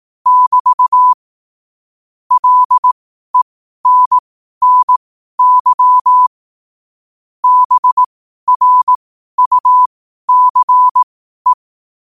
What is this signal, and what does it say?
Morse code, heard as a keyed tone, about 18 wpm, spaced as for 12 wpm, 1000 Hz. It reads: <BT> LENNY BRUCE
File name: qotdfiles / 12wpm